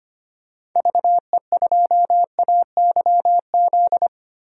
Morse code, heard as a keyed tone, about 25 wpm, 700 Hz. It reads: VE2AYZ
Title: DXCC-100 Call Signs - 6 Characters - Repeated 3x - 25wpm